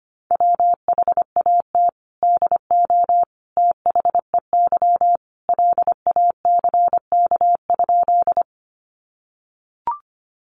Morse code, heard as a keyed tone, about 25 wpm, 700 Hz. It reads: WHAT DO THEY LACK?
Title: Sentences from Top 1100 Words 25wpm